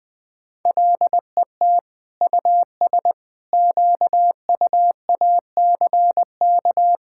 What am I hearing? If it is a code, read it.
Morse code, heard as a keyed tone, about 20 wpm, 700 Hz. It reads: LET US QUACK
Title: All-Age Appropriate Jokes - 20wpm